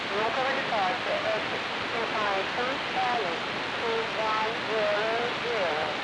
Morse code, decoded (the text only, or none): none